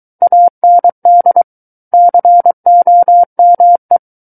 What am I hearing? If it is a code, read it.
Morse code, heard as a keyed tone, about 23 wpm, 700 Hz. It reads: AND COME
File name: Book One / 14-Chapter 14 - 23 WPM